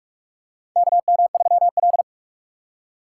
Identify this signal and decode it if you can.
Morse code, heard as a keyed tone, about 46 wpm, 700 Hz. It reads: KM3L
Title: USA Call Signs - 4 Characters - 6x Character Spacing - 45wpm